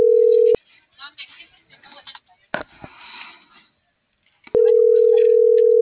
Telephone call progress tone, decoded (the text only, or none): ringback tone